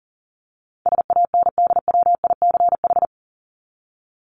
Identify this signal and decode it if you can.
Morse code, heard as a keyed tone, about 40 wpm, 700 Hz. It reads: SANDWICH
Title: Top 1751-2000 Words 40wpm